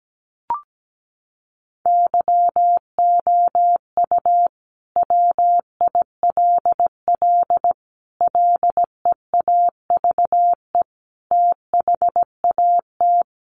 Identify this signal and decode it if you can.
Morse code, heard as a keyed tone, about 17 wpm, 700 Hz. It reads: YOU WILL LEAVE THAT
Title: Sentences from Top 900 Words 17wpm